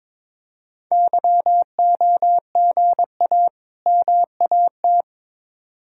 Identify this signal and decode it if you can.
Morse code, heard as a keyed tone, about 22 wpm, 700 Hz. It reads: YOGA MAT
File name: Difficult Words with Hints - 22wpm